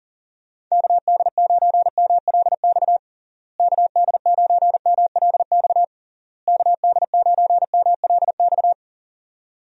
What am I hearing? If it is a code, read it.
Morse code, heard as a keyed tone, about 37 wpm, 700 Hz. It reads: KD9MLX KD9MLX KD9MLX
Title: USA Call Signs - 6 Characters - Repeated 3x - 40wpm